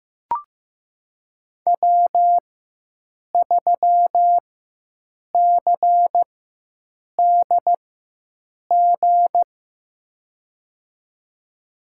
Morse code, heard as a keyed tone, about 15 wpm, spaced as for 7 wpm, 700 Hz. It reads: W3CDG
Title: USA Call Signs - 5 Characters - 4x Character Spacing - 15wpm